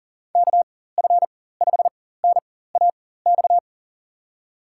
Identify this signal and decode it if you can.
Morse code, heard as a keyed tone, about 40 wpm, spaced as for 19 wpm, 700 Hz. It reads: KF5NAX
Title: USA Call Signs - 4x Character Spacing - 40wpm